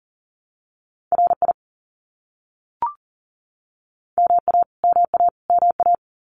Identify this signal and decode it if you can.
Morse code, heard as a keyed tone, about 40 wpm, 700 Hz. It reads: RI MA MA MA
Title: US States Repeated 3x 40wpm